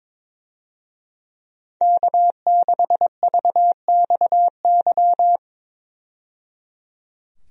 Morse code, heard as a keyed tone, about 22 wpm, 700 Hz. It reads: K6VXY
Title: USA Call Signs - 5 Characters - 22wpm